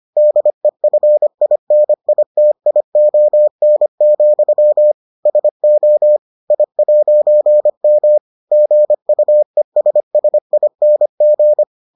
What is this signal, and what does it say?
Morse code, heard as a keyed tone, about 25 wpm, 600 Hz. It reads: DEFINITION, SO I'M GUESSING